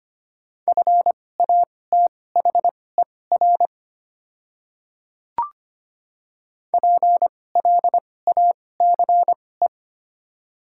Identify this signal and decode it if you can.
Morse code, heard as a keyed tone, about 25 wpm, spaced as for 18 wpm, 700 Hz. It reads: FATHER PLACE
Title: 2x Character Spacing - Top 300 Words - 25wpm